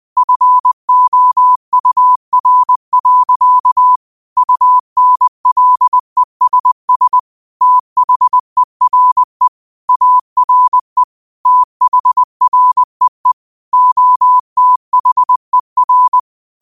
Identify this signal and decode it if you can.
Morse code, heard as a keyed tone, about 20 wpm, 1000 Hz. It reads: FOUR. UNLESS THERE ARE THREE OTHER
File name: qotdfiles / 20wpm